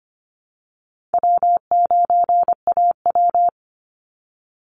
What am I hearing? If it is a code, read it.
Morse code, heard as a keyed tone, about 25 wpm, 700 Hz. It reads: W9AW
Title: Call Signs - 25wpm